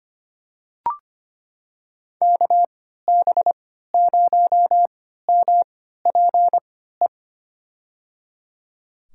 Morse code, heard as a keyed tone, about 25 wpm, spaced as for 14 wpm, 700 Hz. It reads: KB0MPE
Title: USA Call Signs - 3x Character Spacing - 25wpm